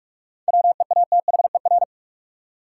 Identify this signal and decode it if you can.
Morse code, heard as a keyed tone, about 43 wpm, 700 Hz. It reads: WEATHER
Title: Seven Letter Words - 45wpm